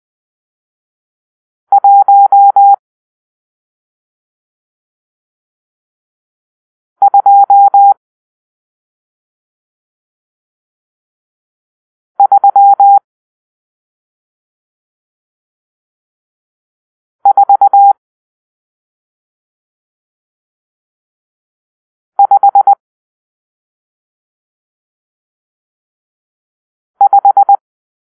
Morse code, heard as a keyed tone, about 20 wpm, 800 Hz. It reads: 1 2 3 4 5 5
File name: arrl / Grp-6 Drill-1